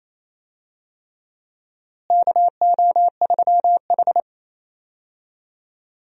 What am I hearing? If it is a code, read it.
Morse code, heard as a keyed tone, about 27 wpm, 700 Hz. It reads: KO3H
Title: USA Call Signs - 4 Characters - Repeated 3x - 28wpm